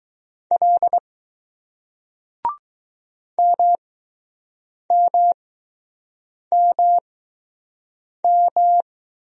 Morse code, heard as a keyed tone, about 20 wpm, spaced as for 6 wpm, 700 Hz. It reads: L MMMM